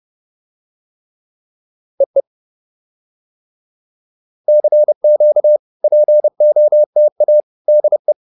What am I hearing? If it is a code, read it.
Morse code, heard as a keyed tone, about 30 wpm, 600 Hz. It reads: EE CQ POTA DE